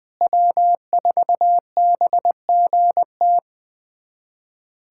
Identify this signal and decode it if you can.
Morse code, heard as a keyed tone, about 20 wpm, 700 Hz. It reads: W4BGT
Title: USA Callsign DE Callsign - 5 Characters - 20wpm